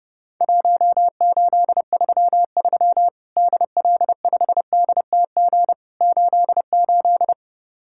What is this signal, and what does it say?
Morse code, heard as a keyed tone, about 29 wpm, 700 Hz. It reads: 1833 DL5DTG 88
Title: Worked All Europe DX Contest - QTC Exchange - 30wpm